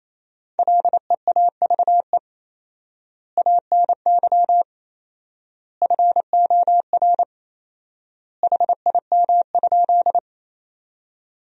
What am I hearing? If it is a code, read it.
Morse code, heard as a keyed tone, about 28 wpm, 700 Hz. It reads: LEAVE ANY FOR HIM?
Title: 4x Word-Spacing - Sentences from Top 400 Words - 28wpm